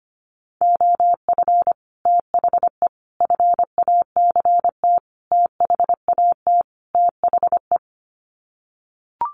Morse code, heard as a keyed tone, about 25 wpm, 700 Hz. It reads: OF THE FACT THAT THE
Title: Sets of 5 Words Encore2 25wpm